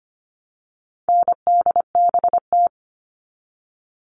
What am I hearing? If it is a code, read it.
Morse code, heard as a keyed tone, about 25 wpm, 700 Hz. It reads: NDBT
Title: All Cut Numbers - 4-Digit 25wpm